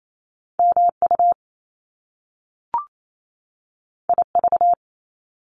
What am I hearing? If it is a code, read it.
Morse code, heard as a keyed tone, about 28 wpm, 700 Hz. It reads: MU IV
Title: Top Two Letter Combinations in Words 28wpm